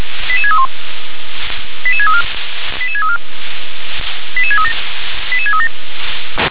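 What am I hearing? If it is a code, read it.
Five-tone selective call (ZVEI), heard as quick runs of five tones, tones of about 70 ms. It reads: 88631 88634 88634 88637 88637
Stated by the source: ZVEI five-tone call 88636 received by SDR.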